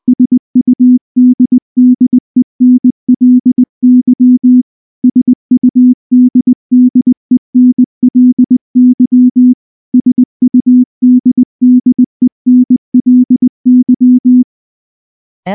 Morse code, heard as a keyed tone, about 19 wpm, 250 Hz. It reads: SUDDENLY SUDDENLY SUDDENLY
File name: morse / morse-training-20WPM-5